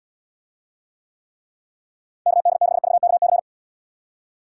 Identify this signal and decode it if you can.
Morse code, heard as a keyed tone, about 79 wpm, 700 Hz. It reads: KD6VKC